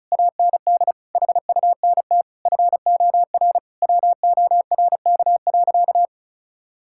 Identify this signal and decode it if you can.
Morse code, heard as a keyed tone, about 34 wpm, 700 Hz. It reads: AND HUNT FOR WORK.
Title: Sentences from Top 1600 Words 35wpm